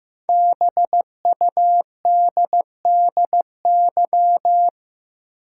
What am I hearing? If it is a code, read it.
Morse code, heard as a keyed tone, about 15 wpm, 700 Hz. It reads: BUDDY
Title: Five Letter Words - 15wpm